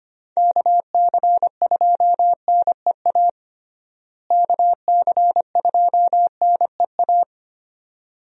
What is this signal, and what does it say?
Morse code, heard as a keyed tone, about 25 wpm, 700 Hz. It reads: KC2NEA KC2NEA
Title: USA Call Signs - 6 Characters - Repeated 2x - 25wpm